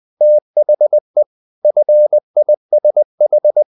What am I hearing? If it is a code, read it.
Morse code, heard as a keyed tone, about 20 wpm, 600 Hz. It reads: THE FISH
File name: the old man and the sea part 2 20wpm